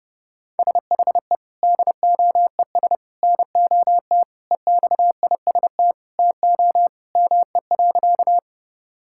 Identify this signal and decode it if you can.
Morse code, heard as a keyed tone, about 30 wpm, 700 Hz. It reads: SHE DOES NOT EXIST TO ME.
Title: Sentences from Top 1000 Words - Review Part 5 - 30wpm